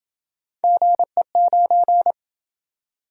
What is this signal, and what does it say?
Morse code, heard as a keyed tone, about 27 wpm, 700 Hz. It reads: GE9